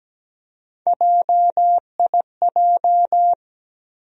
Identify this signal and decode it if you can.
Morse code, heard as a keyed tone, about 17 wpm, 700 Hz. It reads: JIJ
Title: USA Call Sign Postfixes - 3 Letters 17wpm